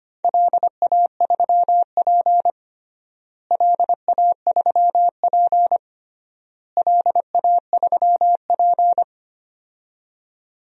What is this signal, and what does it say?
Morse code, heard as a keyed tone, about 25 wpm, 700 Hz. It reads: LA3P LA3P LA3P
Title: DXCC-100 Call Signs - 4 Characters - Repeated 3x - 25wpm